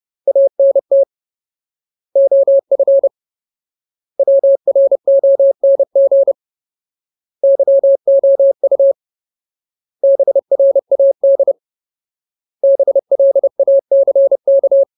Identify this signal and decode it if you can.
Morse code, heard as a keyed tone, about 30 wpm, 550 Hz. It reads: ANT OF WRONG YOU BRAD BLACK
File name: Ultimate Word List / 30WPM Ultimate